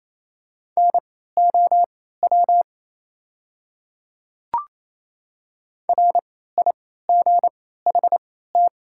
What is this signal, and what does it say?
Morse code, heard as a keyed tone, about 27 wpm, spaced as for 16 wpm, 700 Hz. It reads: NOW RIGHT
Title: 3x Character Spacing - Top 200 Words - 28wpm